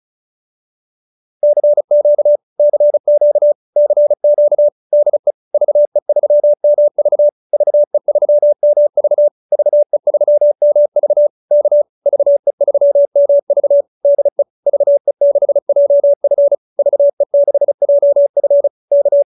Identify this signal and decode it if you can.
Morse code, heard as a keyed tone, about 34 wpm, 590 Hz. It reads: CQ CQ CQ DE VE3MV VE3MV VE3MV K VE3MV DE VE6JF VE6JF K